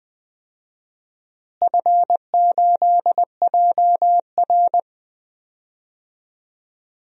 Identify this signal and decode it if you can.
Morse code, heard as a keyed tone, about 20 wpm, 700 Hz. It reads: F8JR
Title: DXCC-100 Call Signs - 4 Characters - Repeated 3x - 20wpm